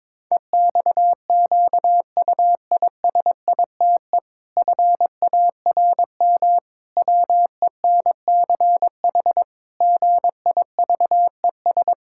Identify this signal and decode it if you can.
Morse code, heard as a keyed tone, about 22 wpm, 700 Hz. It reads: EXQUISITE FARM WENCH GIVES
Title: Pangram Challenge v2 - 22wpm